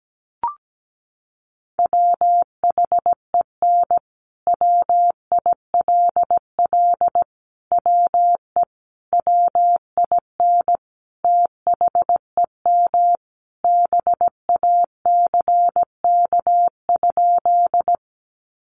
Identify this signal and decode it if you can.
Morse code, heard as a keyed tone, about 17 wpm, 700 Hz. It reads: WHEN WILL WE WIN THEM BACK?
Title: Sentences from Top 1500 Words 17wpm